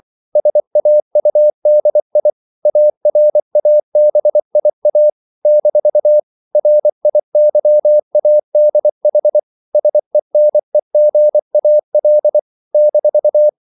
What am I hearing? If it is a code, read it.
Morse code, heard as a keyed tone, about 24 wpm, 600 Hz. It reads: SAUDI ARABIA - RIYADH SENEGAL -
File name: CW / Practice / NationalCapitals24